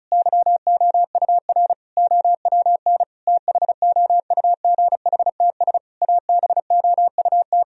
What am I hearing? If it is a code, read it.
Morse code, heard as a keyed tone, about 36 wpm, 700 Hz. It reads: YOUR OWN THOUGHTS ABOUT
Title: Sentences from Top 1000 Words - Review Part 1 - 35wpm